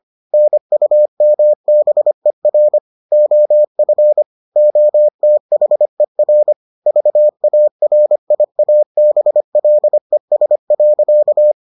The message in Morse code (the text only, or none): NUMBER OF OTHER VARIABLES.